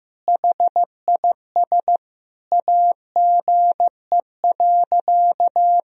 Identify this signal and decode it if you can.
Morse code, heard as a keyed tone, about 15 wpm, 700 Hz. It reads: HIS AGE.
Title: Sentences from Top 1000 Words - Review Part 3 - 15wpm